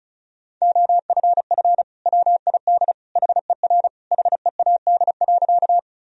Morse code, heard as a keyed tone, about 36 wpm, 700 Hz. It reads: OFF WID HER HEAD.
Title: Sentences from Top 300 Words and Abbreviations 35wpm